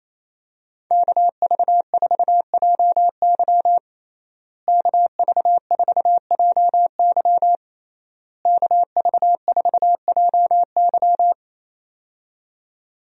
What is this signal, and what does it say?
Morse code, heard as a keyed tone, about 28 wpm, 700 Hz. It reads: KV4JY KV4JY KV4JY
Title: USA Call Signs - 5 Characters - Repeated 3x - 28wpm